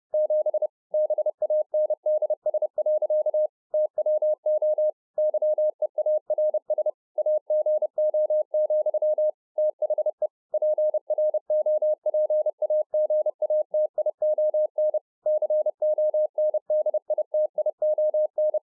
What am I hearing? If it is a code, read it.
Morse code, heard as a keyed tone, about 30 wpm, 610 Hz. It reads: 7 BANDS. TWO YEARS AGO, THE PROPAGATION CONDITION